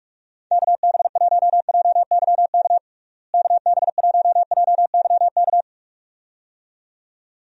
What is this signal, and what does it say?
Morse code, heard as a keyed tone, about 46 wpm, 700 Hz. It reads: KB1JYK KB1JYK
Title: USA Call Signs - 6 Characters - Repeated 2x - 45wpm